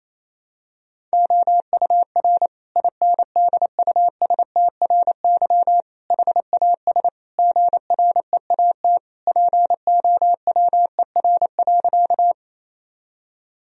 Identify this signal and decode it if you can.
Morse code, heard as a keyed tone, about 28 wpm, 700 Hz. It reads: OUR INDUSTRY HAS GREAT POWER.
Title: Sentences from Top 1000 Words - Review Part 4 - 28wpm